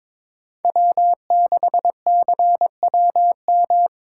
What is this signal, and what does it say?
Morse code, heard as a keyed tone, about 22 wpm, 700 Hz. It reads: W6CWM
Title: K1USN SST QSO Practice - 22wpm